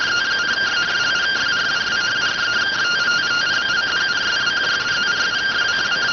Two-tone teleprinter signal, about 170 Hz shift.